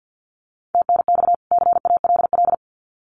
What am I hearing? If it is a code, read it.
Morse code, heard as a keyed tone, about 50 wpm, 700 Hz. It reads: TNX CALL
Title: Common QSO Phrases 50wpm